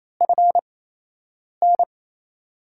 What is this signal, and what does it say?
Morse code, heard as a keyed tone, about 28 wpm, spaced as for 8 wpm, 700 Hz. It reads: FN
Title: USA Call Signs - 8x Character Spacing - 28wpm